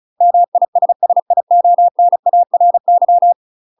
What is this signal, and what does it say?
Morse code, heard as a keyed tone, about 36 wpm, 700 Hz. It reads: MISSIONARY
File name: UK MASTER QSOs TRAINING / 35WPM / UK MASTER QSOs 0018